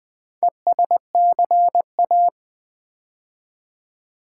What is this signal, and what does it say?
Morse code, heard as a keyed tone, about 20 wpm, 700 Hz. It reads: ESCA
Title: Build A Word - 2 Letters At A Time - 8-Letter Words - 20wpm